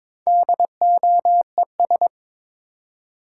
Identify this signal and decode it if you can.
Morse code, heard as a keyed tone, about 22 wpm, 700 Hz. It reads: DOES
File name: Build A Phrase - 2-Word Phrases - 22wpm